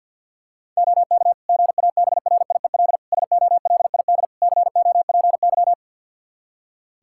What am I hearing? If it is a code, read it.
Morse code, heard as a keyed tone, about 47 wpm, 700 Hz. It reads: OK GABRIEL SOLID COPY
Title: Practice QSO - Exchange 3 - More Info - 50wpm